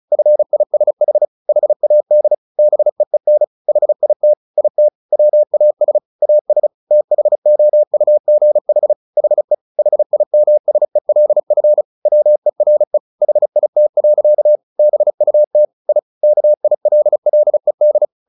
Morse code, heard as a keyed tone, about 34 wpm, 600 Hz. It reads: FISH HAD BEEN HIT IT WAS AS THOUGH HE HIMSELF WERE HIT. BUT I KILLED
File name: the old man and the sea 2 35wpm